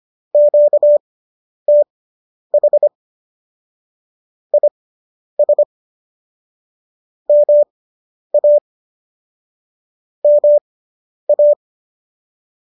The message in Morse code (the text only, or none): QTH IS MA MA